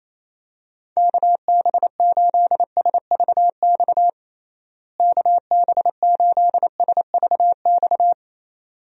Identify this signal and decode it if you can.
Morse code, heard as a keyed tone, about 28 wpm, 700 Hz. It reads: KB8SVX KB8SVX
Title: USA Call Signs - 6 Characters - Repeated 2x - 28wpm